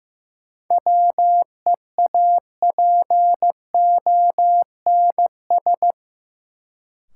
Morse code, heard as a keyed tone, about 15 wpm, 700 Hz. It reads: WEAPONS
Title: Sets of 5 Words Encore - New Word Drill - 15wpm